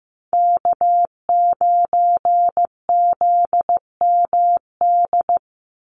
Morse code, heard as a keyed tone, about 15 wpm, 700 Hz. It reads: K9ZMD
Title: NA SOTA Chaser Callsigns 15wpm